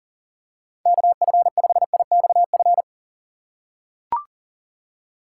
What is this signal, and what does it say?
Morse code, heard as a keyed tone, about 40 wpm, 700 Hz. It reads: KF5IXF
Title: USA Call Signs - 6 Characters - 2x Character Spacing - 40wpm